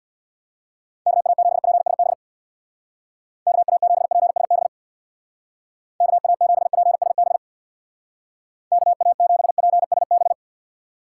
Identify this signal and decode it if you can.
Morse code, heard as a keyed tone, about 60 wpm, 700 Hz. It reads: KA7PSB KA7PSB KA7PSB KA7PSB
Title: Speed Racing - USA Call Signs 50wpm